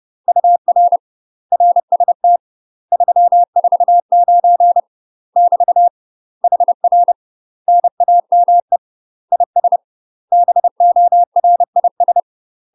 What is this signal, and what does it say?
Morse code, heard as a keyed tone, about 30 wpm, 700 Hz. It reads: UR RST 349 <BT> HR NAME IS BORIS
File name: UK MASTER QSOs TRAINING / 30WPM / UK MASTER QSOs 0002